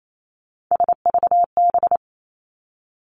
Morse code, heard as a keyed tone, about 28 wpm, 700 Hz. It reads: SVB
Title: Three Challenging Letter-Number Combinations 28wpm